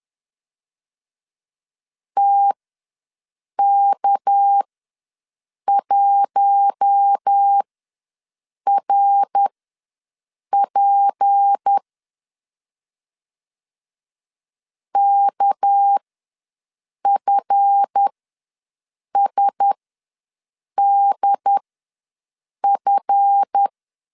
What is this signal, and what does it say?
Morse code, heard as a keyed tone, about 10 wpm, spaced as for 6 wpm, 800 Hz. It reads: TK1RP KFSDF
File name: LEZIONI / 7 GF9Z / 7 GF ws 10wpm 3918